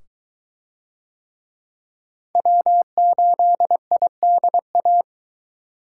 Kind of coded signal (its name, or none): Morse code